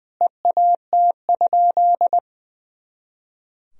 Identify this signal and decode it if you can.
Morse code, heard as a keyed tone, about 21 wpm, 700 Hz. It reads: EAT?
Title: Sentences from Top 300 Words and Abbreviations 20wpm